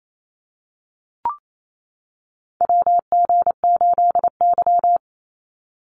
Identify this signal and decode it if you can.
Morse code, heard as a keyed tone, about 28 wpm, 700 Hz. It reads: WG8Y
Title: NA SOTA Chaser Callsigns 28wpm